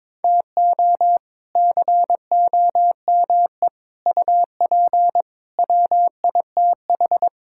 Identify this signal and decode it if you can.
Morse code, heard as a keyed tone, about 22 wpm, 700 Hz. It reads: TO COME UP WITH
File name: practice / 22wpm / Sets of 7 Words - 22wpm